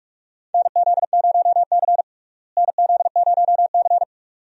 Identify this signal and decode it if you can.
Morse code, heard as a keyed tone, about 46 wpm, 700 Hz. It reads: NZ0C NZ0C
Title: USA Call Signs - 4 Characters - Repeated 2x - 45wpm